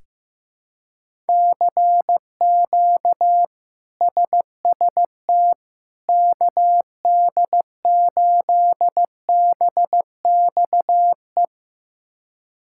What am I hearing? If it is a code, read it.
Morse code, heard as a keyed tone, about 15 wpm, 700 Hz. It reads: CQ SST KD8BXE
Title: K1USN SST QSO Practice - 15wpm